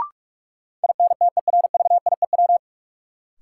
Morse code, heard as a keyed tone, about 46 wpm, 700 Hz. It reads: INTERVIEW
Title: Sentences from Top 1600 Words - New Word Drill Alt - 45wpm